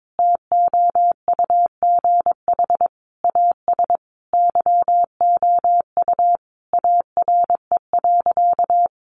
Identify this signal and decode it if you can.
Morse code, heard as a keyed tone, about 22 wpm, 700 Hz. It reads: TOUGH AS YOU ARE.
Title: Sentences from Top 1200 Words 22wpm